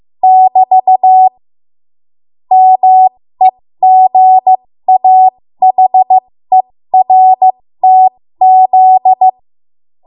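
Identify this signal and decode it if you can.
Morse code, heard as a keyed tone, about 15 wpm, spaced as for 13 wpm, 750 Hz. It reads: <BT> MEGAHERTZ
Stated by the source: 13 wpm Morse code practice archive transmission.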